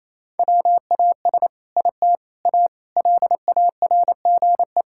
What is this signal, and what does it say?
Morse code, heard as a keyed tone, about 28 wpm, 700 Hz. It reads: WAS IT A LARGE
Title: Sentences from Top 1000 Words - Review Part 5 - 28wpm